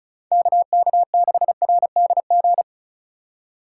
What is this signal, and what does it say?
Morse code, heard as a keyed tone, about 43 wpm, 700 Hz. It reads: KK6RDG
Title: Speed Racing - USA 2x3 Call Signs 35wpm